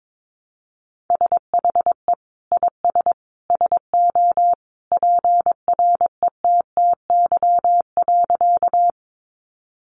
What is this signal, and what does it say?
Morse code, heard as a keyed tone, about 22 wpm, 700 Hz. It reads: SHE IS SO PRETTY.
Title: Sentences from Top 1000 Words 22wpm